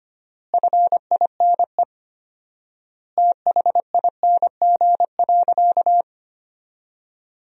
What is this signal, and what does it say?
Morse code, heard as a keyed tone, about 25 wpm, 700 Hz. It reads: FINE THING.
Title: 4x Word-Spacing - Sentences from Top 500 Words - 25wpm